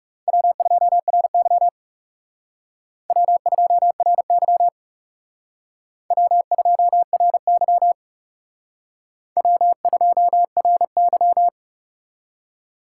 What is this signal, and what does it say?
Morse code, heard as a keyed tone, about 37 wpm, 700 Hz. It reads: W2RY W2RY W2RY W2RY
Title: Speed Racing - USA 1x2 Call Signs 30wpm